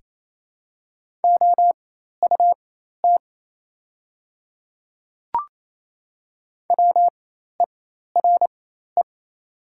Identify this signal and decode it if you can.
Morse code, heard as a keyed tone, about 28 wpm, spaced as for 13 wpm, 700 Hz. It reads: OUT WERE
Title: 4x Character Spacing - Top 200 Words - 28wpm